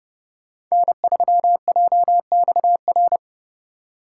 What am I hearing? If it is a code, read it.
Morse code, heard as a keyed tone, about 31 wpm, 700 Hz. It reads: N3JXR
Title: USA Call Signs - 5 Characters - 4x Character Spacing - 30wpm